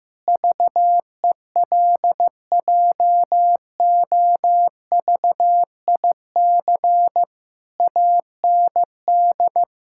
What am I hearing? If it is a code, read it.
Morse code, heard as a keyed tone, about 14 wpm, 700 Hz. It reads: VELJOVIC AND
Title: Pangram Challenge v2 - 15wpm